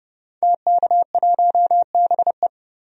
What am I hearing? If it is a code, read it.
Morse code, heard as a keyed tone, about 30 wpm, 700 Hz. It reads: TK1BE
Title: DXCC-100 Call Signs - 5 Characters - 30wpm